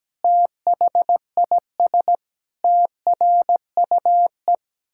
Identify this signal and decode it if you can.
Morse code, heard as a keyed tone, about 17 wpm, 700 Hz. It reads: THIS TRUE
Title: Pangram Challenge v2 - 17wpm